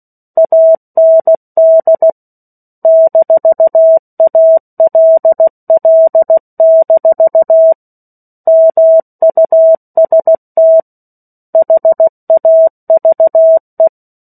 Morse code, heard as a keyed tone, about 16 wpm, 650 Hz. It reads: AND -ALL- MUST HAVE
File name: 03-Chapter-03 16 WPM